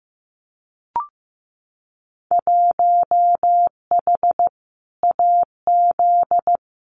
Morse code, heard as a keyed tone, about 15 wpm, 700 Hz. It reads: 1H AZ
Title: Winter Field Day Exchanges 15wpm